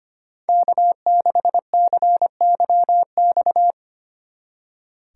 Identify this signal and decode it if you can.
Morse code, heard as a keyed tone, about 32 wpm, 700 Hz. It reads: K6CYX